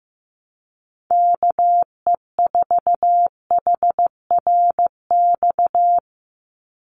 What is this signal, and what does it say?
Morse code, heard as a keyed tone, about 15 wpm, 700 Hz. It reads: KE4HRX
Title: USA Call Signs 15wpm